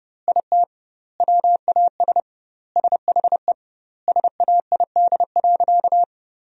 Morse code, heard as a keyed tone, about 30 wpm, 700 Hz. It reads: IT WAS SHE SAID.
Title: 2x Word-Spacing - Sentences from Top 400 Words - 30wpm